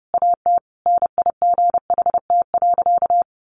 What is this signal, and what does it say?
Morse code, heard as a keyed tone, about 29 wpm, 700 Hz. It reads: AT NIGHT.
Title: Sentences from Top 1300 Words 30wpm